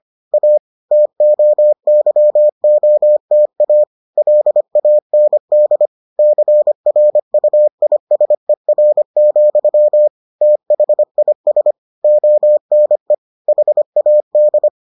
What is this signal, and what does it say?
Morse code, heard as a keyed tone, about 25 wpm, 600 Hz. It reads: A TOYOTA LAND CRUISER, THIS ONE HAD